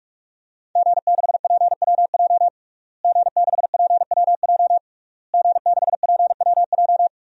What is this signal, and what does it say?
Morse code, heard as a keyed tone, about 46 wpm, 700 Hz. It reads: G6PWJ G6PWJ G6PWJ
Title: DXCC-100 Call Signs - 5 Characters - Repeated 3x - 45wpm